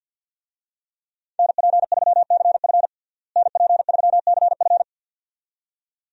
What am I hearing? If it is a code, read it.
Morse code, heard as a keyed tone, about 52 wpm, 700 Hz. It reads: NP3CF NP3CF
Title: USA Call Signs - 5 Characters - Repeated 2x - 50wpm